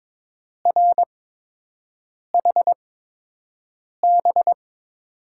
Morse code, heard as a keyed tone, about 21 wpm, spaced as for 6 wpm, 700 Hz. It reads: RHB
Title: USA Call Signs - 6 Characters - 8x Character Spacing - 22wpm